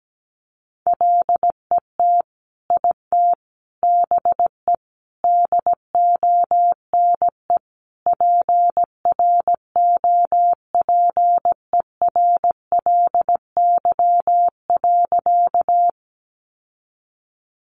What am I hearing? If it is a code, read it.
Morse code, heard as a keyed tone, about 17 wpm, 700 Hz. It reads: LET IT BE DONE PROPERLY.
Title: Sentences from Top 1100 Words 17wpm